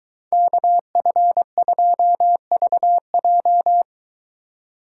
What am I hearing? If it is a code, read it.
Morse code, heard as a keyed tone, about 23 wpm, 700 Hz. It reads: KF2VJ